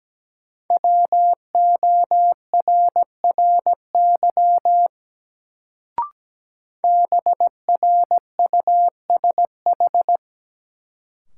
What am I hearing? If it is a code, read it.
Morse code, heard as a keyed tone, about 17 wpm, 700 Hz. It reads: WORRY BRUSH
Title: Sentences from Top 1100 Words - New Word Drill Alt - 17wpm